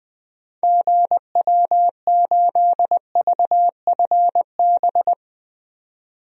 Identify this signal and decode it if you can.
Morse code, heard as a keyed tone, about 20 wpm, 700 Hz. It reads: GW8VFB
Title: DXCC-100 Call Signs - 6 Characters - 20wpm